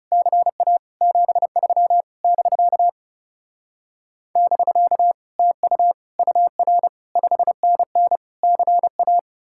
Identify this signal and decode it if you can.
Morse code, heard as a keyed tone, about 40 wpm, 700 Hz. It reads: CA 73 <BK> <BK> TU UR 5NN CA